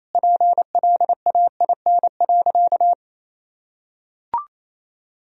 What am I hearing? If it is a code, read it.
Morse code, heard as a keyed tone, about 28 wpm, 700 Hz. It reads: PLAIN.
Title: Sentences from Top 1000 Words - Review Part 2 - 28wpm